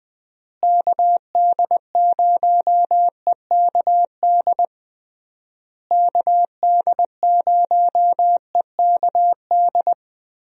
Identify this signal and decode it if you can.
Morse code, heard as a keyed tone, about 21 wpm, 700 Hz. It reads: KD0EKD KD0EKD
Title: USA Call Signs - 6 Characters - Repeated 3x - 20wpm